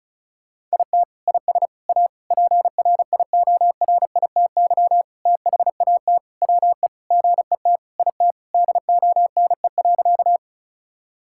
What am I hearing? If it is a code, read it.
Morse code, heard as a keyed tone, about 34 wpm, 700 Hz. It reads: IT IS A PRIORITY THAT WE GET IT DONE.
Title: Sentences from Top 1600 Words 35wpm